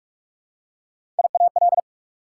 Morse code, heard as a keyed tone, about 46 wpm, 690 Hz. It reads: IAL
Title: Prefixes and Suffixes 45wpm